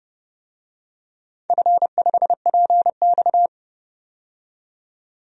Morse code, heard as a keyed tone, about 31 wpm, 700 Hz. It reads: F5PX
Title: DXCC-100 Call Signs - 4 Characters - 30wpm